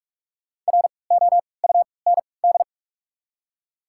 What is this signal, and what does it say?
Morse code, heard as a keyed tone, about 42 wpm, spaced as for 25 wpm, 700 Hz. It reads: ROUND
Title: 3x Character Spacing - Top 200 Words - 45wpm